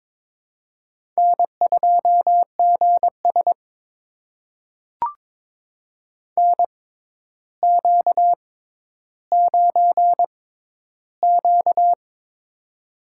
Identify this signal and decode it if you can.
Morse code, heard as a keyed tone, about 22 wpm, spaced as for 11 wpm, 700 Hz. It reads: N2GS NQ9Q
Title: USA Call Signs - 4 Characters - 6x Character Spacing - 22wpm